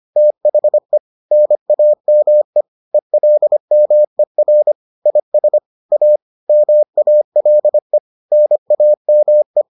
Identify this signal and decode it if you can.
Morse code, heard as a keyed tone, about 25 wpm, 600 Hz. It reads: THE NAME ELMER IS A MALE NAME